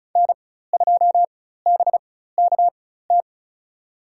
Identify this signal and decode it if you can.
Morse code, heard as a keyed tone, about 36 wpm, spaced as for 16 wpm, 700 Hz. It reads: N2BKT
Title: USA Call Signs - 5 Characters - 4x Character Spacing - 35wpm